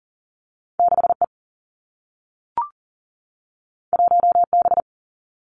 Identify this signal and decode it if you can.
Morse code, heard as a keyed tone, about 40 wpm, 700 Hz. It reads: 6 E 1 B
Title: Two Challenging Letter-Number Combinations 40wpm